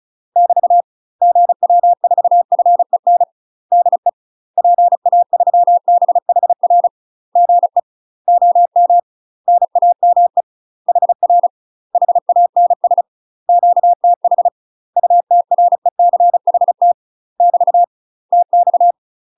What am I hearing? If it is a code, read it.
Morse code, heard as a keyed tone, about 36 wpm, 700 Hz. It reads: <BT> GW4FEN DE PA3BHR GE OM NAME HR HANS QTH UTRECHT <BT> TX
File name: UK MASTER QSOs TRAINING / 35WPM / UK MASTER QSOs 0015